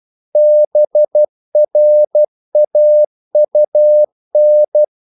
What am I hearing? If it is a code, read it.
Morse code, heard as a keyed tone, about 12 wpm, 600 Hz. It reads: BRAUN
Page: English quotations in CW (telegraphy - morse)